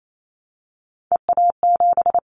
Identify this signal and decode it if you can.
Morse code, heard as a keyed tone, about 28 wpm, 700 Hz. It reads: EA7
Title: Callsign Prefixes for Top 25 DXCC - 28wpm